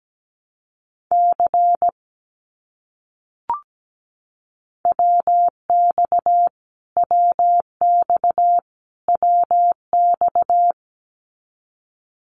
Morse code, heard as a keyed tone, about 17 wpm, 700 Hz. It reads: C WX WX WX
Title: Abbreviations Repeated 3x - 17wpm